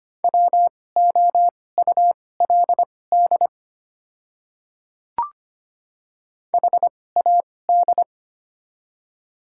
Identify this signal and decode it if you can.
Morse code, heard as a keyed tone, about 25 wpm, spaced as for 18 wpm, 700 Hz. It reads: WOULD HAD
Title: 2x Character Spacing - Top 100 Words - 25wpm